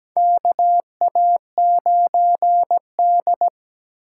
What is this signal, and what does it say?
Morse code, heard as a keyed tone, about 17 wpm, 700 Hz. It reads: KA9D
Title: USA Call Signs - 4 Characters - 6x Character Spacing - 17wpm